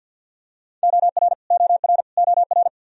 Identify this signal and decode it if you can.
Morse code, heard as a keyed tone, about 52 wpm, 700 Hz. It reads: OR OR OR
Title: Top 100 Words Repeated 50wpm